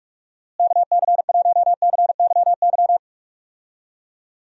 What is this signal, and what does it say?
Morse code, heard as a keyed tone, about 47 wpm, 700 Hz. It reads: KC1CYY
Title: USA Call Signs - 6 Characters - Repeated 2x - 45wpm